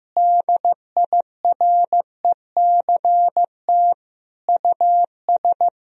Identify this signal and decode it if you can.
Morse code, heard as a keyed tone, about 16 wpm, 700 Hz. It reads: DIRECT US
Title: Sentences from Top 1000 Words - Review Part 2 - 15wpm